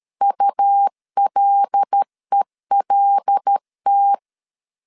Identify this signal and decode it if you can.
Morse code, heard as a keyed tone, about 13 wpm, 800 Hz. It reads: ULELT
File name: CW / DET / 1 6 1500C S A 12wpm 3440